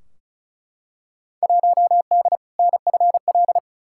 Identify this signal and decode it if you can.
Morse code, heard as a keyed tone, about 36 wpm, 700 Hz. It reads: 1D NFL
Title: Field Day Exchanges - 35wpm